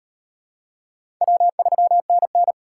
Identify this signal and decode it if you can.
Morse code, heard as a keyed tone, about 34 wpm, 700 Hz. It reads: W3NN